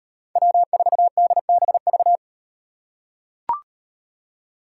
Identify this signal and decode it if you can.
Morse code, heard as a keyed tone, about 35 wpm, 700 Hz. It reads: W4DBV